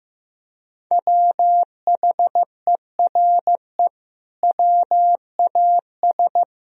Morse code, heard as a keyed tone, about 16 wpm, 700 Hz. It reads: WHERE WAS
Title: Sentences from Top 1000 Words - Review Part 1 - 15wpm